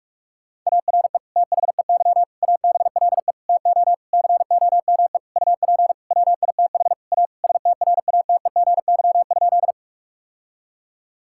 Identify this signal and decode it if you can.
Morse code, heard as a keyed tone, about 41 wpm, 700 Hz. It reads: ARE THEY ABLE TO COME UP WITH A STRATEGY?
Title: Sentences from Top 1000 Words - Review Part 5 - 45wpm